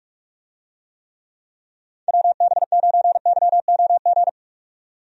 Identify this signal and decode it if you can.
Morse code, heard as a keyed tone, about 46 wpm, 700 Hz. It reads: WB9YOG
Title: USA Call Signs - 6 Characters - 45wpm